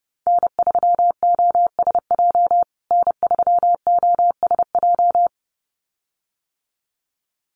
Morse code, heard as a keyed tone, about 30 wpm, 700 Hz. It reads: N3OSJ N3OSJ
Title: USA Call Signs Repeated 2x 30wpm